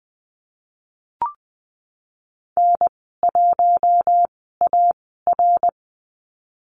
Morse code, heard as a keyed tone, about 19 wpm, spaced as for 14 wpm, 700 Hz. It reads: N1AR
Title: 062 Lesson 11g - Letter h - Call Signs 20wpm